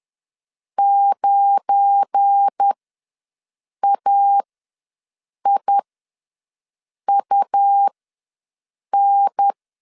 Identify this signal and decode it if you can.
Morse code, heard as a keyed tone, about 10 wpm, spaced as for 6 wpm, 800 Hz. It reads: 9AIUN
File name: LEZIONI / 7 GF9Z / 7 GF9Z ws 10wpm 3919